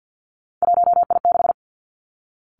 Morse code, heard as a keyed tone, about 50 wpm, 700 Hz. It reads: 1I6
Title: Three Challenging Letter-Number Combinations 50wpm